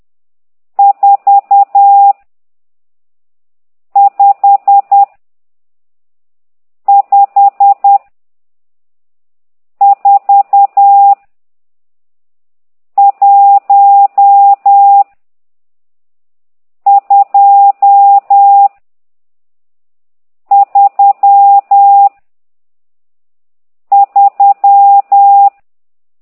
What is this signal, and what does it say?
Morse code, heard as a keyed tone, about 10 wpm, 800 Hz. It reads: 4 5 5 4 1 2 3 3
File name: ct2idw / exerc12